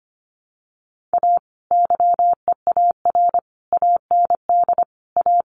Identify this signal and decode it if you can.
Morse code, heard as a keyed tone, about 25 wpm, 700 Hz. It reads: A YEAR AND A
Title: Sets of 5 Words 25wpm